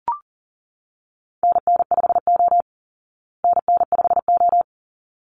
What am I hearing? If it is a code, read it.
Morse code, heard as a keyed tone, about 40 wpm, 700 Hz. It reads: NN5O NN5O
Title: Call Signs Repeated 3x - 40wpm